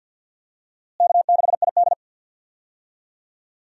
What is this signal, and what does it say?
Morse code, heard as a keyed tone, about 51 wpm, 700 Hz. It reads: K6ID
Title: USA Call Signs - 4 Characters - Repeated 2x - 50wpm